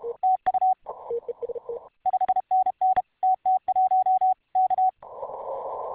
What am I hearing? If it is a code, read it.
Morse code, heard as a keyed tone, about 33 wpm, 750 Hz. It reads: TU 5NN TT1 K